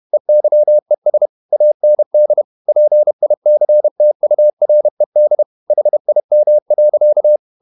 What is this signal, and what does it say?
Morse code, heard as a keyed tone, about 31 wpm, 600 Hz. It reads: EYES AND PICTURED HIM.
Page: Never 31wpm | CW med Gnister